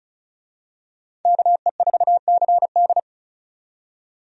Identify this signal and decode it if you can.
Morse code, heard as a keyed tone, about 34 wpm, 700 Hz. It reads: KE4CD
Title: USA Call Signs - 5 Characters - 35wpm